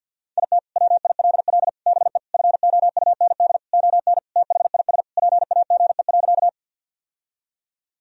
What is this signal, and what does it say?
Morse code, heard as a keyed tone, about 51 wpm, 700 Hz. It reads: IT WILL BE FOUND ON THIS PAGE.
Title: Sentences from Top 1000 Words - Review Part 1 - 50wpm